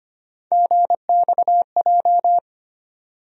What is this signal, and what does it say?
Morse code, heard as a keyed tone, about 25 wpm, 700 Hz. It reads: GXJ
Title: USA Call Sign Postfixes - 3 Letters 25wpm